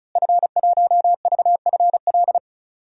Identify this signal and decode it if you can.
Morse code, heard as a keyed tone, about 34 wpm, 700 Hz. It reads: F1VFL
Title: DXCC-100 Call Signs - 5 Characters - Repeated 3x - 35wpm